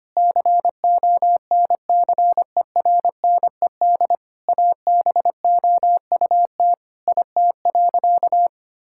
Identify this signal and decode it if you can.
Morse code, heard as a keyed tone, about 25 wpm, 700 Hz. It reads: CONCERNED ABOUT IT.
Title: Sentences from Top 1000 Words - Review Part 6 - 25wpm